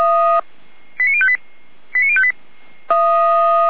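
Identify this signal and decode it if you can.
Five-tone selective call (ZVEI), heard as quick runs of five tones, tones of about 70 ms. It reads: 89058 89058
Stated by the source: ZVEI five-tone selective call 89207 received on SDR.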